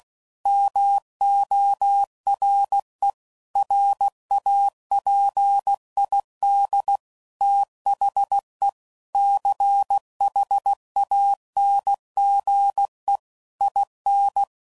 Morse code, heard as a keyed tone, about 16 wpm, 800 Hz. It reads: MORE RAPID THE CHANGE IN
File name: CW Code Groups @15WPM / Technical Subjects / XC